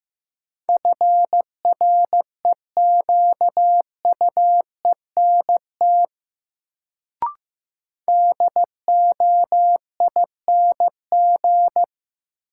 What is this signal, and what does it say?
Morse code, heard as a keyed tone, about 15 wpm, 700 Hz. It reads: FREQUENT DOING
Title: Sentences from Top 1400 Words - New Word Drill Alt - 15wpm